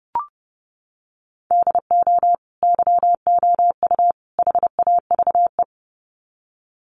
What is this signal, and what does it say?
Morse code, heard as a keyed tone, about 30 wpm, 700 Hz. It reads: DO YOU HAVE
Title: Sets of 3 Words 30wpm